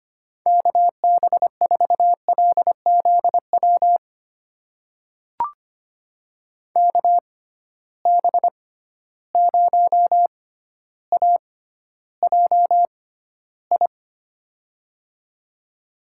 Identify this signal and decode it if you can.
Morse code, heard as a keyed tone, about 25 wpm, spaced as for 13 wpm, 700 Hz. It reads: KB4LZW KB0AJI
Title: USA Call Signs - 6 Characters - 6x Character Spacing - 25wpm